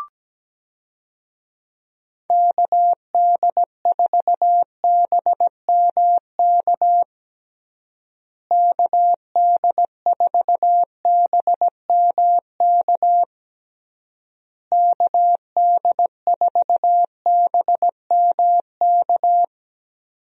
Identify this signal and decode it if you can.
Morse code, heard as a keyed tone, about 17 wpm, 700 Hz. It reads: KD4BMK KD4BMK KD4BMK
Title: USA Call Signs - 6 Characters - Repeated 3x - 17wpm